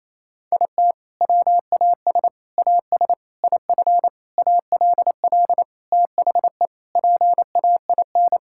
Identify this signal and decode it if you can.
Morse code, heard as a keyed tone, about 28 wpm, 700 Hz. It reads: IT WAS AS IF ALL THE PAIN
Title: Sentences from Top 1600 Words 28wpm